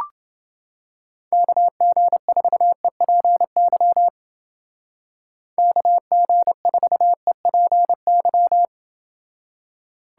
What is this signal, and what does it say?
Morse code, heard as a keyed tone, about 28 wpm, 700 Hz. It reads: KG4EPY KG4EPY